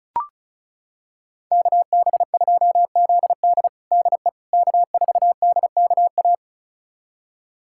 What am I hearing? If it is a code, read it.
Morse code, heard as a keyed tone, about 36 wpm, 700 Hz. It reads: KB2ZD DE K4DKA
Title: USA Callsign DE Callsign - 5 Characters - 35wpm